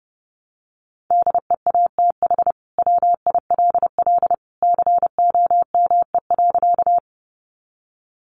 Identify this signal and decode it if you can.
Morse code, heard as a keyed tone, about 30 wpm, 700 Hz. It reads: DEATH WILL COME.
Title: Sentences from Top 1100 Words 30wpm